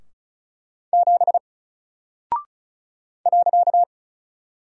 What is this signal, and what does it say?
Morse code, heard as a keyed tone, about 36 wpm, 700 Hz. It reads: 7 .
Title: Single Character (Letter:Number:Punctuation:Prosign) 35wpm